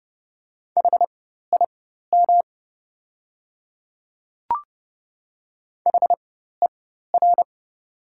Morse code, heard as a keyed tone, about 30 wpm, spaced as for 14 wpm, 700 Hz. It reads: HIM HER
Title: 4x Character Spacing - Top 100 Words - 30wpm